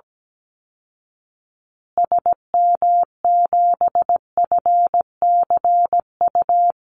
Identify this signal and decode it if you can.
Morse code, heard as a keyed tone, about 17 wpm, 700 Hz. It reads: SM7FCU
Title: Call Signs Repeated 2x - 17wpm